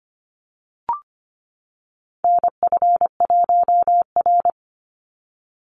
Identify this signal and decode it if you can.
Morse code, heard as a keyed tone, about 25 wpm, 700 Hz. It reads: NF1R
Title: NA SOTA Chaser Callsigns 25wpm